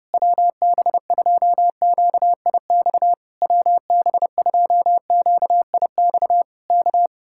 Morse code, heard as a keyed tone, about 32 wpm, 700 Hz. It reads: WB2QIX WB2QIX K